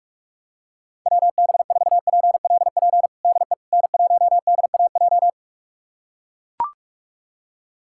Morse code, heard as a keyed tone, about 41 wpm, 700 Hz. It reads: WB4PLP DE N1DAJ
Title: USA Callsign DE Callsign 45wpm